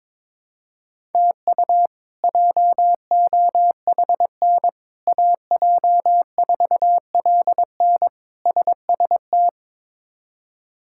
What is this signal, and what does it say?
Morse code, heard as a keyed tone, about 22 wpm, 700 Hz. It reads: TU JOHN AJ4LN SST
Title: K1USN SST QSO Practice - 22wpm